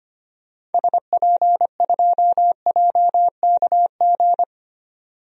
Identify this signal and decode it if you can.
Morse code, heard as a keyed tone, about 25 wpm, 700 Hz. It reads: SP2JKG
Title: DXCC-100 Call Signs - 6 Characters - 25wpm